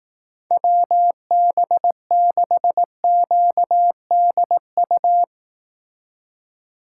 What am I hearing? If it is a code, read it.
Morse code, heard as a keyed tone, about 18 wpm, 700 Hz. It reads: WB6QDU
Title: Speed Racing - USA 2x3 Call Signs 15wpm